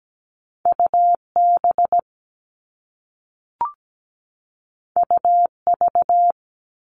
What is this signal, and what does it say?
Morse code, heard as a keyed tone, about 17 wpm, 700 Hz. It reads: UB UV
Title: All Cut Numbers - 2-Digit 17wpm